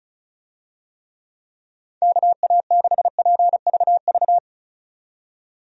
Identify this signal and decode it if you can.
Morse code, heard as a keyed tone, about 36 wpm, 700 Hz. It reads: KA6PVV
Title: USA Call Signs - 6 Characters - 35wpm